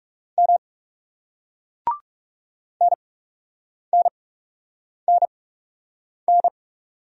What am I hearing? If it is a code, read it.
Morse code, heard as a keyed tone, about 42 wpm, spaced as for 8 wpm, 700 Hz. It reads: M NNNN